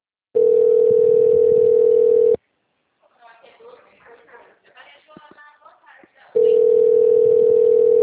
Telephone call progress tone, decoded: ringback tone

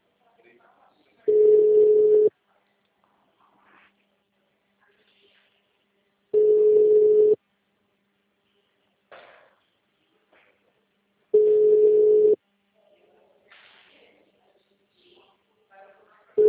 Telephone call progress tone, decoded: ringback tone

